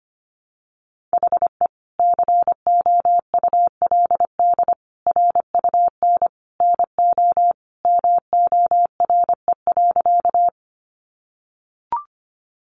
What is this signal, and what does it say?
Morse code, heard as a keyed tone, about 25 wpm, 700 Hz. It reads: HE COULD RUN NO MORE.
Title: Sentences from Top 300 Words 25wpm